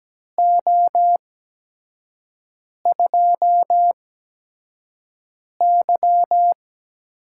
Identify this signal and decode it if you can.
Morse code, heard as a keyed tone, about 17 wpm, spaced as for 5 wpm, 700 Hz. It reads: O2Y
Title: USA Call Signs - 4 Characters - 8x Character Spacing - 17wpm